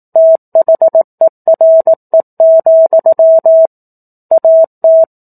Morse code, heard as a keyed tone, about 18 wpm, 650 Hz. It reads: THERE, AT
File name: 04-Chapter-04 18 WPM